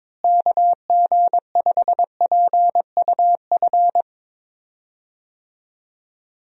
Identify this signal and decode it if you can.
Morse code, heard as a keyed tone, about 22 wpm, 700 Hz. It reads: KG5PUF
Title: USA Call Signs - 6 Characters - Repeated 2x - 22wpm